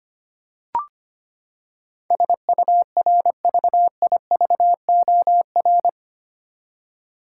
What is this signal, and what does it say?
Morse code, heard as a keyed tone, about 25 wpm, 700 Hz. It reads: SURVIVOR
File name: Eight Letter Words - 25wpm